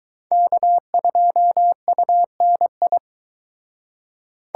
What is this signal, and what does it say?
Morse code, heard as a keyed tone, about 22 wpm, 700 Hz. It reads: K2UNI